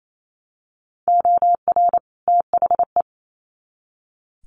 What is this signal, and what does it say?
Morse code, heard as a keyed tone, about 28 wpm, 700 Hz. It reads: OR THE
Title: Sets of 2 Words 28wpm